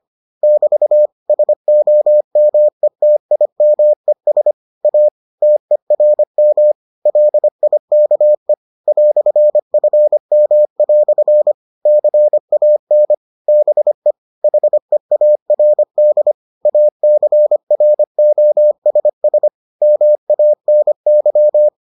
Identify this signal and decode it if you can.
Morse code, heard as a keyed tone, about 25 wpm, 600 Hz. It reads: <BT> SOMETIMES A TERM LIKE "FM" CAN BE HEARD ACROSS MANY